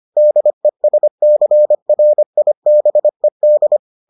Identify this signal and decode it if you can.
Morse code, heard as a keyed tone, about 25 wpm, 600 Hz. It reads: DESCRIBED